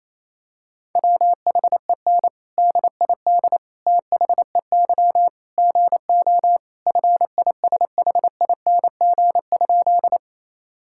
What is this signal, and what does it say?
Morse code, heard as a keyed tone, about 28 wpm, 700 Hz. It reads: WHEN DID THEY GO FISHING?
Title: Sentences from Top 1000 Words - Review Part 5 - 28wpm